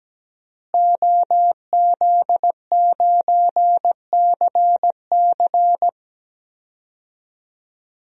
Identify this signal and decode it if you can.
Morse code, heard as a keyed tone, about 17 wpm, 700 Hz. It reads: OZ9CC
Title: DXCC-100 Call Signs - 5 Characters - Repeated 2x - 17wpm